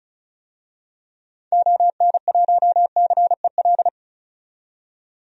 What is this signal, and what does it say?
Morse code, heard as a keyed tone, about 36 wpm, 700 Hz. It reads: ON1CEL
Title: DXCC-100 Call Signs - 6 Characters - 35wpm